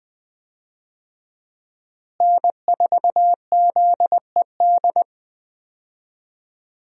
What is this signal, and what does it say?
Morse code, heard as a keyed tone, about 20 wpm, 700 Hz. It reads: N4ZED
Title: USA Call Signs - 5 Characters - Repeated 3x - 20wpm